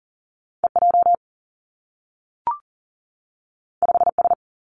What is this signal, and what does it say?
Morse code, heard as a keyed tone, about 38 wpm, 700 Hz. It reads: E J 5 S
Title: Two Challenging Letter-Number Combinations 40wpm